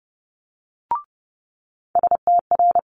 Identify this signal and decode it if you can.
Morse code, heard as a keyed tone, about 30 wpm, 700 Hz. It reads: STR
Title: Multi-letter Phonemes Quick Fire 30wpm